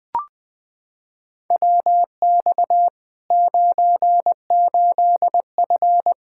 Morse code, heard as a keyed tone, about 21 wpm, 700 Hz. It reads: WX 98F
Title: Practice QSO - Exchange 2 - Info - 20wpm